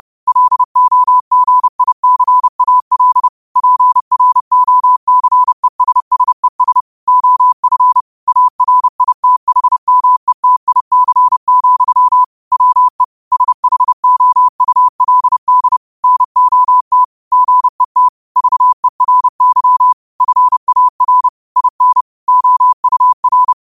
Morse code, heard as a keyed tone, about 30 wpm, 1000 Hz. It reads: LOGICAL PROCESSES OF ARITHMETIC, WE SHOULD NOT GET VERY FAR IN OUR